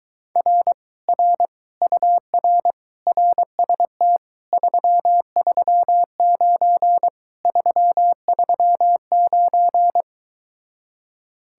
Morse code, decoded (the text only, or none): R R UR RST 339 339